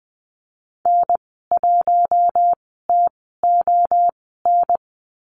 Morse code, heard as a keyed tone, about 20 wpm, spaced as for 14 wpm, 700 Hz. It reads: N1TON
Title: 069 Lesson 12g - Letter d - Call Signs 20wpm